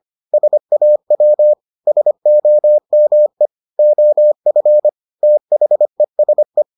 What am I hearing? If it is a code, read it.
Morse code, heard as a keyed tone, about 25 wpm, 600 Hz. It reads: SAW SOME OF THESE